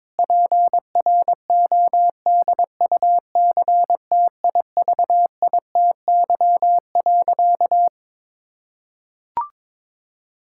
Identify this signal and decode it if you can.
Morse code, heard as a keyed tone, about 22 wpm, 700 Hz. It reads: PRODUCTIVITY.
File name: Difficult Sentences with Hints - 22wpm